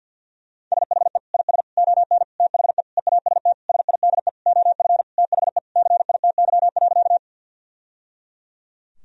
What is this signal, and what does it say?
Morse code, heard as a keyed tone, about 50 wpm, 700 Hz. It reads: SHE IS ON THE EAST SIDE OF THE CITY.
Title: Sentences from Top 1000 Words - Review Part 3 - 50wpm